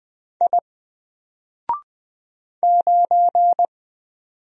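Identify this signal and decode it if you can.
Morse code, heard as a keyed tone, about 20 wpm, 700 Hz. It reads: I 9
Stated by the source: Single letter and number Morse drill at 20 wpm, from N4LCD.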